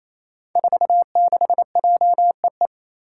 Morse code, heard as a keyed tone, about 28 wpm, 700 Hz. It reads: 46JEE
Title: Five Challenging Letter-Number Combinations 28wpm